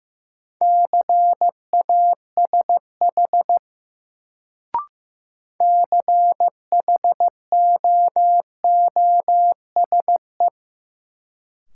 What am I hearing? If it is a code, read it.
Morse code, heard as a keyed tone, about 15 wpm, 700 Hz. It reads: CASH CHOOSE
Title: Sentences from Top 700 Words - New Word Drill - 15wpm